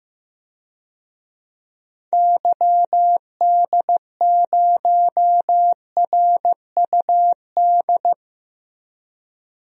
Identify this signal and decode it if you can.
Morse code, heard as a keyed tone, about 15 wpm, 700 Hz. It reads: YD0RUD
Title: DXCC-100 Call Signs - 6 Characters - Repeated 2x - 15wpm